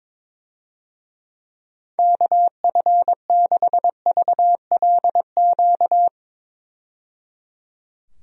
Morse code, heard as a keyed tone, about 22 wpm, 700 Hz. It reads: KF6VLQ
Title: USA Call Signs - 6 Characters - 22wpm